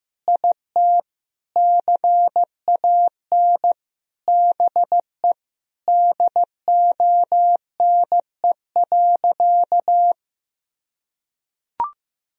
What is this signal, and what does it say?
Morse code, heard as a keyed tone, about 16 wpm, 700 Hz. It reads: IT CAN BE DONE.
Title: Sentences from Top 1000 Words - Review Part 1 - 15wpm